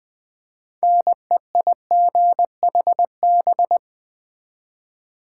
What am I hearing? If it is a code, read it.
Morse code, heard as a keyed tone, about 20 wpm, 700 Hz. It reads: NEIGHB
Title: Build A Word - 3 Letters At A Time - 9-Letter Words - 20wpm